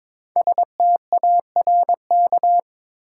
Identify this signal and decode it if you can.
Morse code, heard as a keyed tone, about 22 wpm, 700 Hz. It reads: STARK
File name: 5 Letter Words - First 3 Letters Given - 22wpm